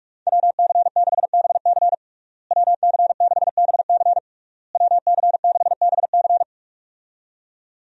Morse code, heard as a keyed tone, about 46 wpm, 700 Hz. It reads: WC6BC WC6BC WC6BC
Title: USA Call Signs - 5 Characters - Repeated 3x - 45wpm